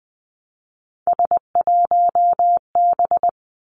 Morse code, heard as a keyed tone, about 20 wpm, 700 Hz. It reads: S1B
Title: Three Challenging Letter-Number Combinations 20wpm